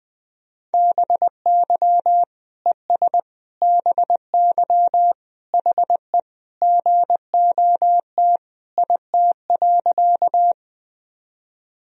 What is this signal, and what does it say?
Morse code, heard as a keyed tone, about 20 wpm, 700 Hz. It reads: BY ES BY HE GOT IT.
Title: Sentences from Top 300 Words and Abbreviations 20wpm